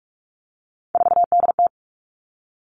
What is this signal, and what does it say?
Morse code, heard as a keyed tone, about 46 wpm, 700 Hz. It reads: 4DT
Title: Three Challenging Letter-Number Combinations 45wpm